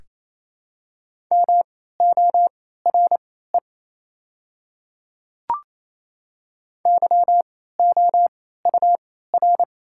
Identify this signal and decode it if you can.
Morse code, heard as a keyed tone, about 28 wpm, spaced as for 16 wpm, 690 Hz. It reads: MORE YOUR
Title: 3x Character Spacing - Top 100 Words - 28wpm